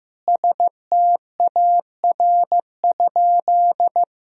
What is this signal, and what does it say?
Morse code, heard as a keyed tone, about 15 wpm, 700 Hz. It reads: STAR?
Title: Sentences from Top 1000 Words - Review Part 2 - 15wpm